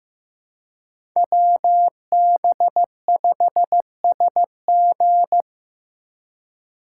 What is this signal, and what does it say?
Morse code, heard as a keyed tone, about 18 wpm, 700 Hz. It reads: WB5SG
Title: Speed Racing - USA 2x2 Call Signs 15wpm